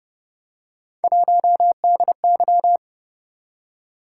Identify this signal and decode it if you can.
Morse code, heard as a keyed tone, about 31 wpm, 700 Hz. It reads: 1DY
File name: Speed-Racing - Three Letter-Number Combinations - 25wpm